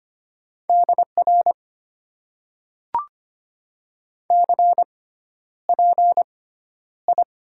Morse code, heard as a keyed tone, about 25 wpm, spaced as for 11 wpm, 700 Hz. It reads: DR CPI
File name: Top 90 QSO Elements - 6x Character Spacing - 25wpm